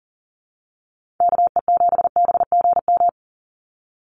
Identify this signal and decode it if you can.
Morse code, heard as a keyed tone, about 40 wpm, 700 Hz. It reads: KE7BGM
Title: NA SOTA Activator Callsigns 40wpm